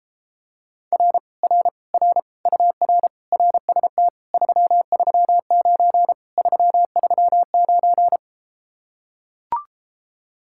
Morse code, heard as a keyed tone, about 33 wpm, 700 Hz. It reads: R R R UR RST 339 339